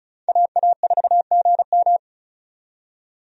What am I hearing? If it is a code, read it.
Morse code, heard as a keyed tone, about 34 wpm, 700 Hz. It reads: AA4GM
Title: USA Call Signs - 5 Characters - 35wpm